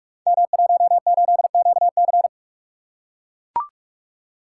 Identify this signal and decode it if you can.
Morse code, heard as a keyed tone, about 44 wpm, 700 Hz. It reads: M18QC
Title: Code Groups - 5 Letter:Numbers - 45wpm